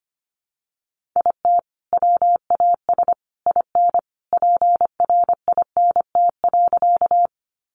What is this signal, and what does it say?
Morse code, heard as a keyed tone, about 25 wpm, 700 Hz. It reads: IT WAS IN PRINT.
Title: Sentences from Top 1400 Words 25wpm